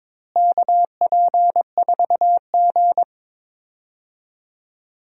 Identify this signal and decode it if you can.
Morse code, heard as a keyed tone, about 22 wpm, 700 Hz. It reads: KP4G
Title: DXCC-100 Call Signs - 4 Characters - Repeated 2x - 22wpm